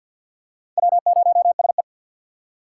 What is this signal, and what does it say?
Morse code, heard as a keyed tone, about 52 wpm, 700 Hz. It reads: W0SE
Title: USA Call Signs - 4 Characters - 3x Character Spacing - 50wpm